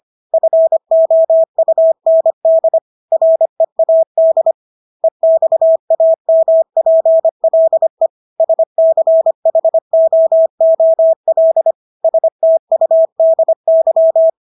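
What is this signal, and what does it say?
Morse code, heard as a keyed tone, about 25 wpm, 650 Hz. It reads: FOUND READ EXAMPLE SCHOOL STUDY
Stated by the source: Morse practice of English words at 25 wpm from CW med Gnister.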